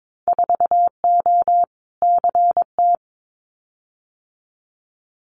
Morse code, heard as a keyed tone, about 22 wpm, 700 Hz. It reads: 4O CT
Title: Winter Field Day Exchanges Encore 22wpm